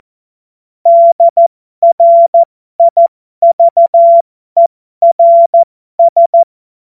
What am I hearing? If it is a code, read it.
Morse code, heard as a keyed tone, about 14 wpm, spaced as for 12 wpm, 680 Hz. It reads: DRIVERS
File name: Morsekurs / Witze